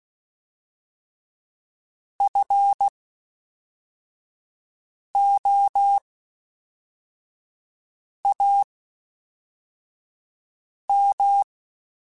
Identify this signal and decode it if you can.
Morse code, heard as a keyed tone, about 16 wpm, spaced as for 4 wpm, 800 Hz. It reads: FOAM
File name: Special INGs / Special INGs-mp3s / Fast ing @10